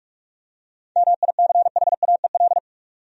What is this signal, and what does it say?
Morse code, heard as a keyed tone, about 46 wpm, 700 Hz. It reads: MICHAEL
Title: USA Common Names - 45wpm